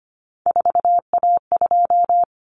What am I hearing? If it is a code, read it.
Morse code, heard as a keyed tone, about 25 wpm, 700 Hz. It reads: 4A2
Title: Common Cut Numbers - 3-Digit 25wpm